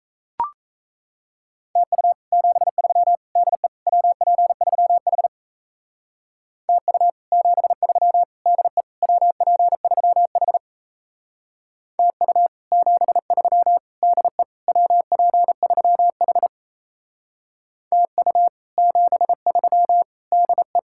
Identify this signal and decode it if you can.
Morse code, heard as a keyed tone, about 34 wpm, 700 Hz. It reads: TU 73 DE WP3H TU 73 DE WP3H TU 73 DE WP3H TU 73 DE
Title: Speed-Racing Common QSO Phrases - 28wpm